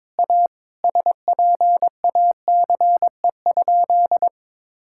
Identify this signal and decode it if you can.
Morse code, heard as a keyed tone, about 22 wpm, 700 Hz. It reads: A SPACE?
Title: Sentences from Top 1000 Words - Review Part 5 - 22wpm